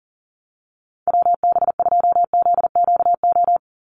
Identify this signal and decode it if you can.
Morse code, heard as a keyed tone, about 40 wpm, 700 Hz. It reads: WB2ZQO
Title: USA Call Signs 40wpm